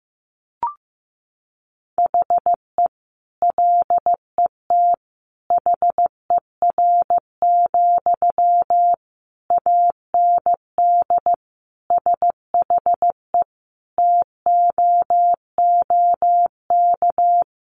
Morse code, heard as a keyed tone, about 15 wpm, 700 Hz. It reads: HE LET HER, AND SHE TOOK
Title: Sentences from Top 1500 Words 15wpm